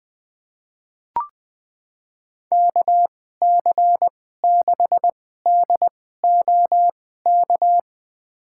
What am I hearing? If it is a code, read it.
Morse code, heard as a keyed tone, about 20 wpm, spaced as for 15 wpm, 700 Hz. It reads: KC6DOK
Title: USA Call Signs - 6 Characters - 2x Character Spacing - 20wpm